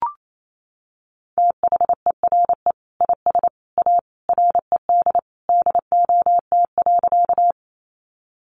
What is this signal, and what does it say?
Morse code, heard as a keyed tone, about 28 wpm, 700 Hz. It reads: THERE IS A RED DOT.
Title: Sentences from Top 1500 Words 28wpm